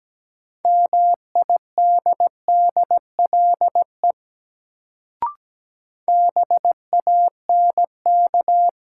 Morse code, heard as a keyed tone, about 17 wpm, 700 Hz. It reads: MIDDLE BANK
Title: Sentences from Top 900 Words - New Word Drill - 17wpm